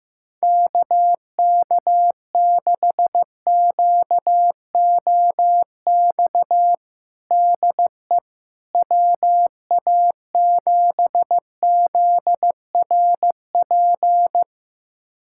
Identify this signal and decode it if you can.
Morse code, heard as a keyed tone, about 15 wpm, 700 Hz. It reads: KK6QOX DE WA7ZRP
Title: USA Callsign DE Callsign - 6 Characters - 15wpm